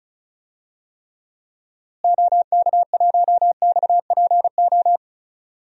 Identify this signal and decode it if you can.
Morse code, heard as a keyed tone, about 34 wpm, 700 Hz. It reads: OK1XPO
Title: DXCC-100 Call Signs - 6 Characters - Repeated 2x - 35wpm